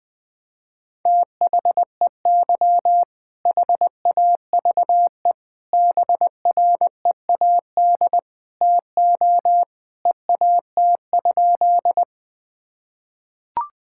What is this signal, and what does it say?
Morse code, heard as a keyed tone, about 20 wpm, 700 Hz. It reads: THEY HAVE BREAD TO EAT?
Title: Sentences from Top 1600 Words 20wpm